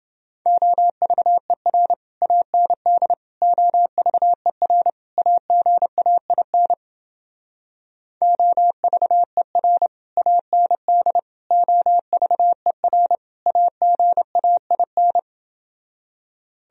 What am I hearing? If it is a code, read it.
Morse code, heard as a keyed tone, about 29 wpm, 700 Hz. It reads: OVER AND OVER AGAIN OVER AND OVER AGAIN